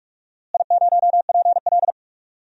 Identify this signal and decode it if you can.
Morse code, heard as a keyed tone, about 41 wpm, 700 Hz. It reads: I0PL
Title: DXCC-100 Call Signs - 4 Characters - Repeated 2x - 45wpm